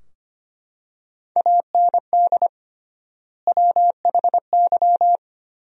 Morse code, heard as a keyed tone, about 25 wpm, 700 Hz. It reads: AND WHY
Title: 3x Word-Spacing - Sentences from Top 300 Words - 25wpm